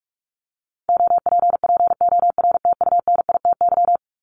Morse code, heard as a keyed tone, about 44 wpm, 700 Hz. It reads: OPPORTUNITY
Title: Top 751-1000 Words 45wpm